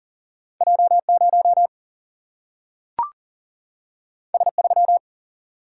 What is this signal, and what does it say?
Morse code, heard as a keyed tone, about 40 wpm, 700 Hz. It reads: J 0 S 3
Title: Single-Letter Single-Number 40wpm